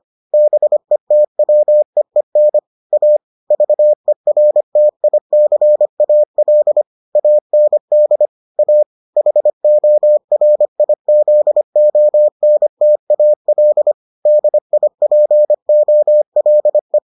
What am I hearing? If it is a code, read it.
Morse code, heard as a keyed tone, about 25 wpm, 600 Hz. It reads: BETWEEN A VERTICAL AND A HORIZONTAL DIPOLE